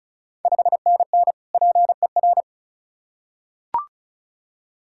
Morse code, heard as a keyed tone, about 36 wpm, 690 Hz. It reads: 5NN PER
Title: Ontario QSO Party Exchange 35wpm